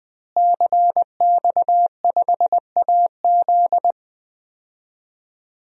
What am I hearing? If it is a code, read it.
Morse code, heard as a keyed tone, about 21 wpm, 700 Hz. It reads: CX5AZ
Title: DXCC-100 Call Signs - 5 Characters - Repeated 3x - 20wpm